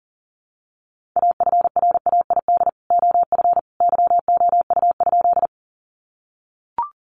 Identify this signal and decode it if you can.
Morse code, heard as a keyed tone, about 40 wpm, 700 Hz. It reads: AFRAID OF YOU?
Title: Sentences from Top 1500 Words 40wpm